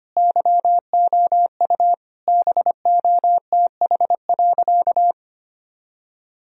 Morse code, heard as a keyed tone, about 25 wpm, 700 Hz. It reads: YOU BOTH.
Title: Sentences from Top 1000 Words - Review Part 1 - 25wpm